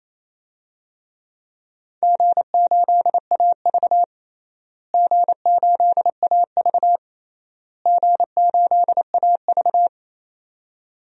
Morse code, heard as a keyed tone, about 28 wpm, 700 Hz. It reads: G8AV G8AV G8AV
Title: DXCC-100 Call Signs - 4 Characters - Repeated 3x - 28wpm